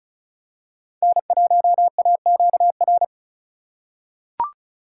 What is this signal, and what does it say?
Morse code, heard as a keyed tone, about 36 wpm, 700 Hz. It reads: N1AQR
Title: USA Call Signs - 5 Characters - 3x Character Spacing - 35wpm